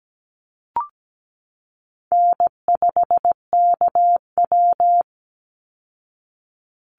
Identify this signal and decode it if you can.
Morse code, heard as a keyed tone, about 17 wpm, 700 Hz. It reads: N5KW
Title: Easy Call Signs Repeated 3x - 17wpm